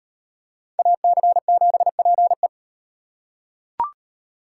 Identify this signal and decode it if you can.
Morse code, heard as a keyed tone, about 38 wpm, 700 Hz. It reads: AC7PE